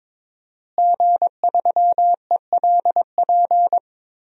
Morse code, heard as a keyed tone, about 22 wpm, 700 Hz. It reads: G3ELP
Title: DXCC-100 Call Signs - 5 Characters - Repeated 3x - 22wpm